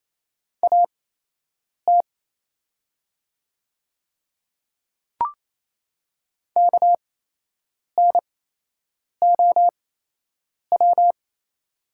Morse code, heard as a keyed tone, about 28 wpm, spaced as for 8 wpm, 700 Hz. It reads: AT KNOW
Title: 8x Character Spacing - Top 100 Words - 28wpm